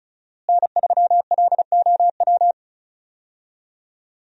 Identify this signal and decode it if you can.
Morse code, heard as a keyed tone, about 36 wpm, 700 Hz. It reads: N3LOW
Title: USA Call Signs - 5 Characters - Repeated 3x - 35wpm